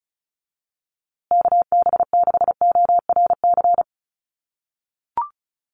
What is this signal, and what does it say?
Morse code, heard as a keyed tone, about 34 wpm, 700 Hz. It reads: KB6ORC
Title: USA Call Signs Repeated 2x 35wpm